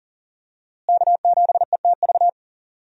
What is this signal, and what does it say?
Morse code, heard as a keyed tone, about 38 wpm, 700 Hz. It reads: K7ETV
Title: USA Call Signs - 5 Characters - Repeated 3x - 40wpm